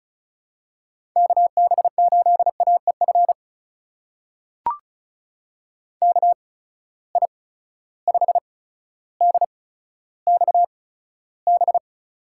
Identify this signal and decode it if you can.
Morse code, heard as a keyed tone, about 36 wpm, spaced as for 15 wpm, 700 Hz. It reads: KB8AEF KI5DXB
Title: USA Call Signs - 6 Characters - 8x Character Spacing - 35wpm